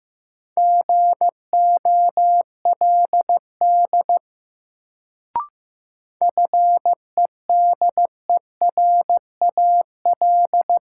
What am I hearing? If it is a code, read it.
Morse code, heard as a keyed tone, about 15 wpm, 700 Hz. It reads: GOLD FEDERAL
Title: Sentences from Top 1100 Words - New Word Drill Alt - 15wpm